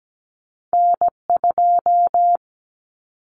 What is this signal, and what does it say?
Morse code, heard as a keyed tone, about 17 wpm, 700 Hz. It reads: N2
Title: Common Cut Numbers - 2-Digit 17wpm